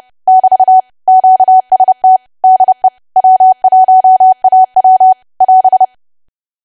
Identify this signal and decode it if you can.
Morse code, heard as a keyed tone, about 30 wpm, 750 Hz. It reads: <BT> QST DE W1AW <AS>
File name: cwa / W1AW-Level3 / 30WPM